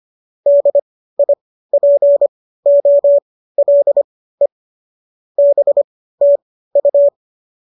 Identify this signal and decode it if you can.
Morse code, heard as a keyed tone, about 25 wpm, spaced as for 15 wpm, 570 Hz. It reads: DIPOLE BTU